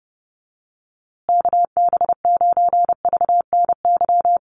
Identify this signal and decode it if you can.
Morse code, heard as a keyed tone, about 30 wpm, 700 Hz. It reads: KB9VNY
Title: USA Call Signs Repeated 3x 30wpm